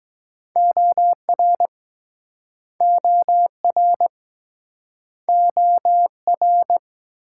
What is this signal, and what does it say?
Morse code, heard as a keyed tone, about 22 wpm, 690 Hz. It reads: OR OR OR